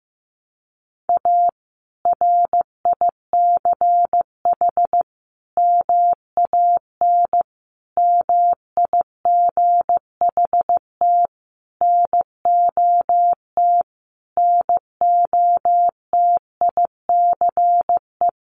Morse code, heard as a keyed tone, about 14 wpm, 700 Hz. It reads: A RICH MAN MIGHT NOT NOTICE
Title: Sentences from Top 1100 Words 15wpm